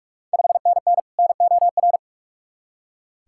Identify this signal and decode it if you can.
Morse code, heard as a keyed tone, about 44 wpm, 690 Hz. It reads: 5NN NOR
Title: Ontario QSO Party Exchange 45wpm